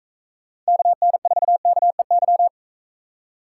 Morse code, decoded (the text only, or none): KN4KEY